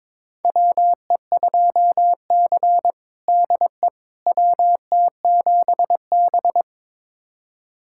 Morse code, heard as a keyed tone, about 21 wpm, 700 Hz. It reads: WE2C DE WT7B
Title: USA Callsign DE Callsign - 4 Characters - 22wpm